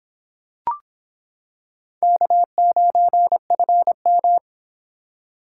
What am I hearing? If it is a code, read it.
Morse code, heard as a keyed tone, about 26 wpm, 700 Hz. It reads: K9FM